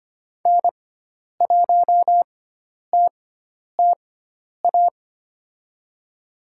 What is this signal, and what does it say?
Morse code, heard as a keyed tone, about 25 wpm, spaced as for 10 wpm, 700 Hz. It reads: N1TTA
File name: Lesson 16g - Letter c - Call Signs 25wpm